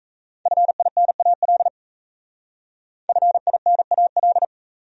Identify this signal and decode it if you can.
Morse code, heard as a keyed tone, about 35 wpm, 700 Hz. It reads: FINAL FINAL
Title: Speed-Racing Top 500 Words 28wpm